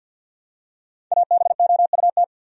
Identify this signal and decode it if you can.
Morse code, heard as a keyed tone, about 51 wpm, 700 Hz. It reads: ABOUT
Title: Top 100 Words 50wpm